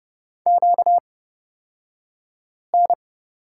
Morse code, heard as a keyed tone, about 31 wpm, spaced as for 5 wpm, 700 Hz. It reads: QN
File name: Lesson 38d - Letter q - Two Characters 30wpm